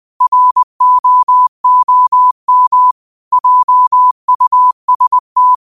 Morse code, heard as a keyed tone, about 20 wpm, 1000 Hz. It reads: ROOM JUST
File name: qotdfiles / 20wpm